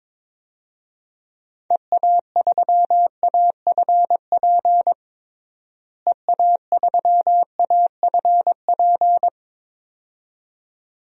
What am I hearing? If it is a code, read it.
Morse code, heard as a keyed tone, about 22 wpm, 700 Hz. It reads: EA3AFP EA3AFP
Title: DXCC-100 Call Signs - 6 Characters - Repeated 2x - 22wpm